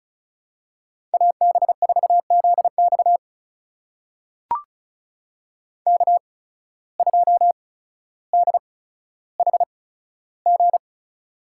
Morse code, heard as a keyed tone, about 36 wpm, spaced as for 15 wpm, 700 Hz. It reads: AB4ZX K2DHG
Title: USA Call Signs - 5 Characters - 8x Character Spacing - 35wpm